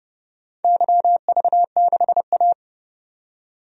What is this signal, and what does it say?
Morse code, heard as a keyed tone, about 29 wpm, 700 Hz. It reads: YV6A
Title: DXCC-100 Call Signs - 4 Characters - 30wpm